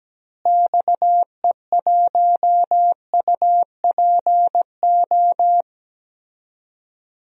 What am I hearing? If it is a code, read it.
Morse code, heard as a keyed tone, about 17 wpm, 700 Hz. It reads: XE1UPO
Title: DXCC-100 Call Signs - 6 Characters - Repeated 3x - 17wpm